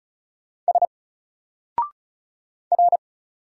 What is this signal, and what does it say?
Morse code, heard as a keyed tone, about 33 wpm, 700 Hz. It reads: S R
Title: Single Character (Letter:Number:Punctuation:Prosign) Alt 35wpm